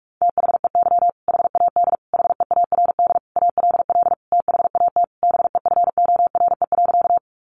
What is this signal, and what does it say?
Morse code, heard as a keyed tone, about 44 wpm, 700 Hz. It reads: THEY HAD HEARD ALL THAT BEFORE.
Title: Sentences from Top 400 Words 45wpm